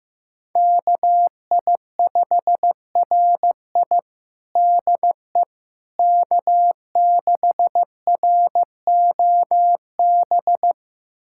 Morse code, heard as a keyed tone, about 15 wpm, 700 Hz. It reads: KI5RI DE K6ROB
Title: USA Callsign DE Callsign - 5 Characters - 15wpm